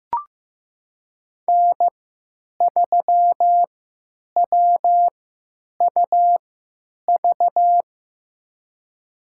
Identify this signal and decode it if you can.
Morse code, heard as a keyed tone, about 15 wpm, spaced as for 9 wpm, 700 Hz. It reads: N3WUV
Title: USA Call Signs - 5 Characters - 3x Character Spacing - 15wpm